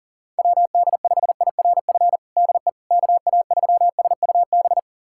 Morse code, heard as a keyed tone, about 40 wpm, 700 Hz. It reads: WD5IRF DE KA3SUB
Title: USA Callsign DE Callsign - 6 Characters - 40wpm